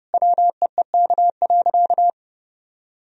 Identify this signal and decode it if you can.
Morse code, heard as a keyed tone, about 30 wpm, 700 Hz. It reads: WEEK.
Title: Sentences from Top 1000 Words - Review Part 2 - 30wpm